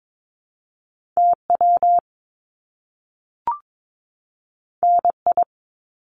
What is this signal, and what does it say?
Morse code, heard as a keyed tone, about 22 wpm, 700 Hz. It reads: TW NI
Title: Top Two Letter Combinations in Words 22wpm